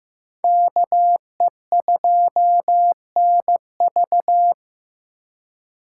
Morse code, heard as a keyed tone, about 18 wpm, 700 Hz. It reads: KE2NV
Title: Speed Racing - USA Call Signs 15wpm